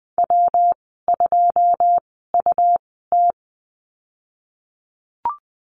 Morse code, heard as a keyed tone, about 20 wpm, spaced as for 15 wpm, 700 Hz. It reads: W2UT
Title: 207 Lesson 33g - Letter j - Call Signs 20wpm